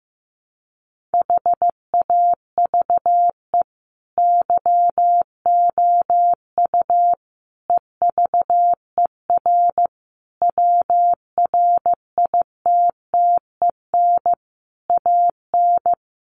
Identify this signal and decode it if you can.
Morse code, heard as a keyed tone, about 14 wpm, 700 Hz. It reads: HAVE YOU EVER WRITTEN AN
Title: Sentences from Top 1500 Words 15wpm